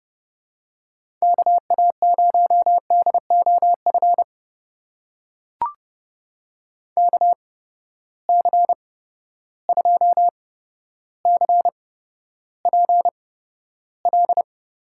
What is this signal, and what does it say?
Morse code, heard as a keyed tone, about 32 wpm, spaced as for 13 wpm, 700 Hz. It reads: KA0DOF KC2CPL
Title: USA Call Signs - 6 Characters - 8x Character Spacing - 30wpm